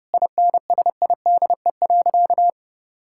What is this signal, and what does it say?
Morse code, heard as a keyed tone, about 30 wpm, 700 Hz. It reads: INSIDE.
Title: Sentences from Top 1000 Words - Review Part 6 - 30wpm